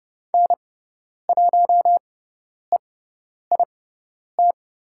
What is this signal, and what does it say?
Morse code, heard as a keyed tone, about 30 wpm, spaced as for 10 wpm, 700 Hz. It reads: N1EIT
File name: Lesson 17g - Letter u - Call Signs 30wpm